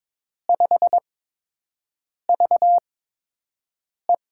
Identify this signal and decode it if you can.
Morse code, heard as a keyed tone, about 22 wpm, spaced as for 6 wpm, 700 Hz. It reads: 5VE
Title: USA Call Signs - 5 Characters - 8x Character Spacing - 22wpm